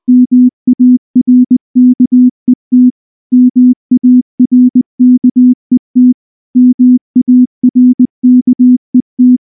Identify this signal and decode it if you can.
Morse code, heard as a keyed tone, about 19 wpm, 250 Hz. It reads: MARKET MARKET MARKET
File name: morse / morse-training-20WPM-8